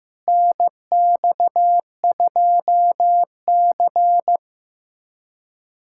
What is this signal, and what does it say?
Morse code, heard as a keyed tone, about 15 wpm, 700 Hz. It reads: NX2C
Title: USA Callsign DE Callsign - 4 Characters - 15wpm